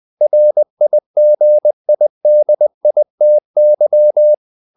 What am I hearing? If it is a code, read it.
Morse code, heard as a keyed tone, about 20 wpm, 600 Hz. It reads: RIGIDITY
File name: the old man and the sea part 2 20wpm